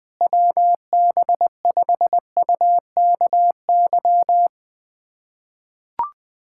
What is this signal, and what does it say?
Morse code, heard as a keyed tone, about 20 wpm, 700 Hz. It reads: WB5UKY
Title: USA Callsign DE Callsign - 6 Characters - 20wpm